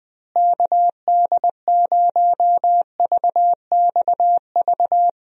Morse code, heard as a keyed tone, about 20 wpm, 700 Hz. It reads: KD0VXV
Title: USA Call Signs - 6 Characters - Repeated 3x - 20wpm